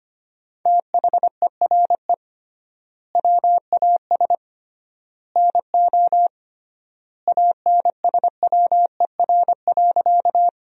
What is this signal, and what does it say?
Morse code, heard as a keyed tone, about 25 wpm, 700 Hz. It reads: THERE WAS NO ANSWER.
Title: 3x Word-Spacing - Sentences from Top 300 Words - 25wpm